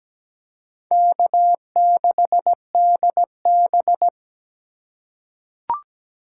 Morse code, heard as a keyed tone, about 17 wpm, 700 Hz. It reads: K6DB
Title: USA Call Signs - 4 Characters - 3x Character Spacing - 17wpm